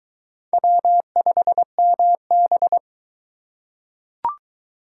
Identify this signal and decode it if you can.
Morse code, heard as a keyed tone, about 23 wpm, 700 Hz. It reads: W5MB